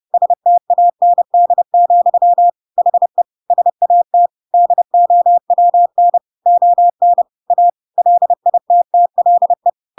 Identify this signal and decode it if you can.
Morse code, heard as a keyed tone, about 30 wpm, 700 Hz. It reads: STAND, HE SAT DOWN ON A LITTLE
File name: Pinocchio / Pinocchio ch6 30 30 ik1hge